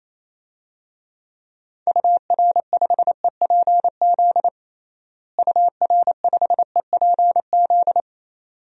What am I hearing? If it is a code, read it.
Morse code, heard as a keyed tone, about 28 wpm, 700 Hz. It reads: UR5EPZ UR5EPZ
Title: DXCC-100 Call Signs - 6 Characters - Repeated 2x - 28wpm